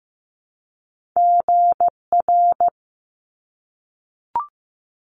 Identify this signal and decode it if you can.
Morse code, heard as a keyed tone, about 15 wpm, 700 Hz. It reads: GR
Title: Top Two Letter Combinations in Words 15wpm